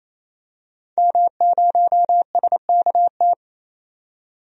Morse code, heard as a keyed tone, about 28 wpm, 700 Hz. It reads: M0SKT
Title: DXCC-100 Call Signs - 5 Characters - 28wpm